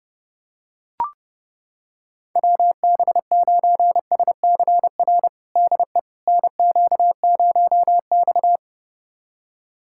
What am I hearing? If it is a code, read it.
Morse code, heard as a keyed tone, about 30 wpm, 700 Hz. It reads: WB9SCR DE NQ0X
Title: USA Callsign DE Callsign 30wpm